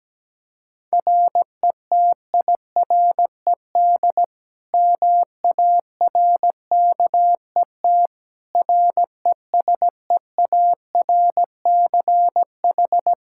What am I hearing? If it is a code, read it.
Morse code, heard as a keyed tone, about 17 wpm, 700 Hz. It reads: RETIRED MARKET RESEARCH
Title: Practice QSO - Exchange 3 - More Info - 17wpm